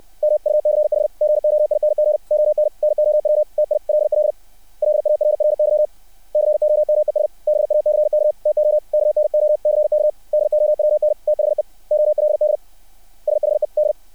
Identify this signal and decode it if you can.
Morse code, heard as a keyed tone, about 60 wpm, 600 Hz. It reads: MUCH GOING ON NOW TILL LARRY COMES BACK TO KNOCK DOWN THE OLD SHED
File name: archived / CRSnet